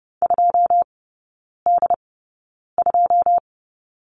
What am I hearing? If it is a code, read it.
Morse code, heard as a keyed tone, about 30 wpm, spaced as for 9 wpm, 700 Hz. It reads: 2D2
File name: 081 Lesson 14e - Number 2 - Three Characters 30wpm